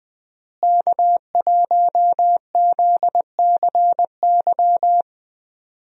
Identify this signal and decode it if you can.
Morse code, heard as a keyed tone, about 20 wpm, 700 Hz. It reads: K1ZCY
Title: USA Call Signs - 5 Characters - Repeated 3x - 20wpm